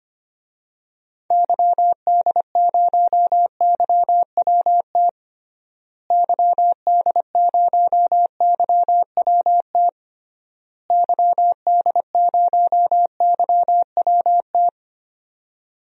Morse code, heard as a keyed tone, about 25 wpm, 700 Hz. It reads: YD0YWT YD0YWT YD0YWT
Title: DXCC-100 Call Signs - 6 Characters - Repeated 3x - 25wpm